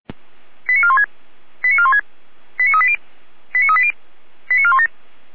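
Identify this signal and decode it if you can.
Five-tone selective call (ZVEI), heard as quick runs of five tones, tones of about 70 ms. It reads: 89416 89416 89380 89380 89417
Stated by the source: Five-tone ZVEI selective call, received with an SDR.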